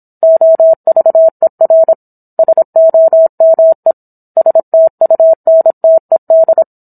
Morse code, heard as a keyed tone, about 26 wpm, 650 Hz. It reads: OVER SOME STUNTED